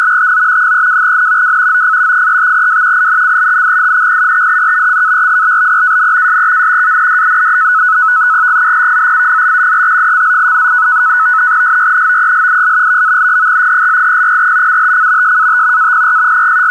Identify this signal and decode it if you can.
NAVTEX (SITOR-B), about 100 baud. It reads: TO 32N W OF 35W INCLUDING THE CARIBBEAN SEA AND GULF OF MEXICO SYNOPSIS VALID 1800 UTC SAT APR
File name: uploader / sitorb 100baud